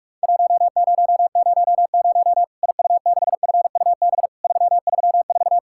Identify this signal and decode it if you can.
Morse code, heard as a keyed tone, about 44 wpm, 700 Hz. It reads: 1000 IU6FUB 334
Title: Worked All Europe DX Contest - QTC Exchange - 45wpm